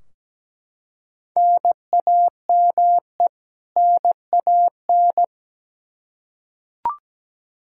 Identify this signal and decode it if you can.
Morse code, heard as a keyed tone, about 17 wpm, 700 Hz. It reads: NAME NAN
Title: Practice QSO - Exchange 1 - Beginning - 17wpm